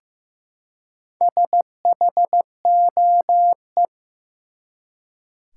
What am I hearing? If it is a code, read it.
Morse code, heard as a keyed tone, about 15 wpm, 700 Hz. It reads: SHOE
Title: Four Letter Words - 15wpm